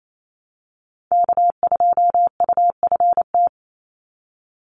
Morse code, heard as a keyed tone, about 28 wpm, 700 Hz. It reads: K2UFT
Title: Call Signs - 28wpm